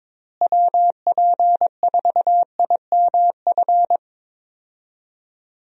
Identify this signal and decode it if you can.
Morse code, heard as a keyed tone, about 22 wpm, 700 Hz. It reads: WP4IMF
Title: USA Call Signs - 6 Characters - Repeated 2x - 22wpm